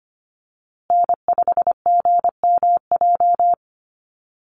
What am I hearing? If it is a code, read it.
Morse code, heard as a keyed tone, about 25 wpm, 700 Hz. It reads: N5GMJ
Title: USA Call Signs Repeated 2x 25wpm